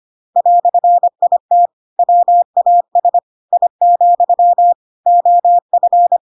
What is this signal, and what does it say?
Morse code, heard as a keyed tone, about 25 wpm, 700 Hz. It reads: "IT WAS I, OF
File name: Pinocchio / Pinocchio ch2 25 25 ik1hge